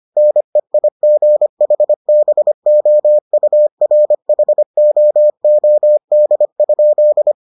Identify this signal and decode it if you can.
Morse code, heard as a keyed tone, about 25 wpm, 600 Hz. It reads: NEIGHBOURHOOD?